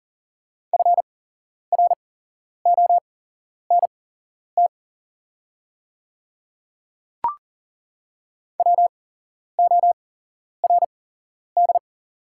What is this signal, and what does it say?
Morse code, heard as a keyed tone, about 40 wpm, spaced as for 11 wpm, 690 Hz. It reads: FRONT WORD
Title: 8x Character Spacing - Top 500 Words - 40wpm